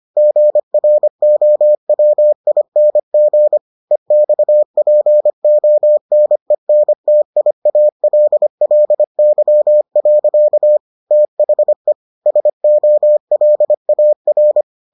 Morse code, heard as a keyed tone, about 25 wpm, 600 Hz. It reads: GROWING EXPONENTIALLY. THE SOLAR